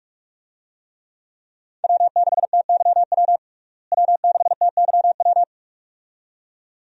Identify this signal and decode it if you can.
Morse code, heard as a keyed tone, about 41 wpm, 700 Hz. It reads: W6TYW W6TYW
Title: USA Call Signs - 5 Characters - Repeated 2x - 45wpm